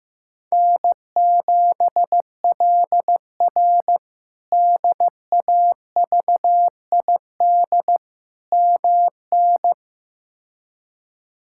Morse code, heard as a keyed tone, about 15 wpm, 700 Hz. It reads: N7LR DAVID MN
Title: K1USN SST QSO Practice - 15wpm